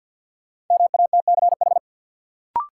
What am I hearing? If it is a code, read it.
Morse code, heard as a keyed tone, about 53 wpm, 700 Hz. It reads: MATCH
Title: Sentences from Top 1600 Words - New Word Drill Alt - 50wpm